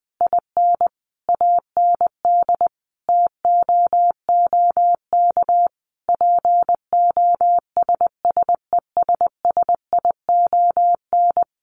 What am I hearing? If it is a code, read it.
Morse code, heard as a keyed tone, about 20 wpm, 700 Hz. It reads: IN AND TOOK POSSESSION
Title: Sentences from Top 1400 Words 20wpm